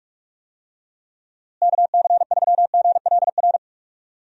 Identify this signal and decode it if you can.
Morse code, heard as a keyed tone, about 46 wpm, 700 Hz. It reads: KC3GLR
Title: USA Call Signs - 6 Characters - Repeated 3x - 45wpm